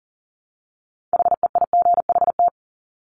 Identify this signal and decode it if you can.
Morse code, heard as a keyed tone, about 41 wpm, 700 Hz. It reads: HEIGHT
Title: Multi-letter Phonemes 40wpm